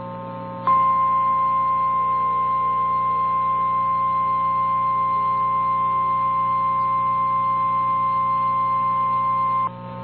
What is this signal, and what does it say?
Emergency Alert System (SAME) tones (the 1050 Hz attention tone), with noise. It is heard from 0.6 to 9.7 s.
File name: eas / KNCYKBIE / KNCYKBIE